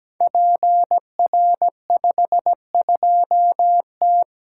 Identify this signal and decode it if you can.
Morse code, heard as a keyed tone, about 17 wpm, 700 Hz. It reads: PR52T
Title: Code Groups - 5 Letter:Numbers - 17wpm